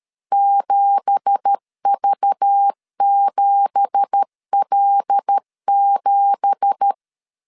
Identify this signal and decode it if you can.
Morse code, heard as a keyed tone, about 13 wpm, 800 Hz. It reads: 7V7L7
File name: LEZIONI / 9 2867 12wpm 2322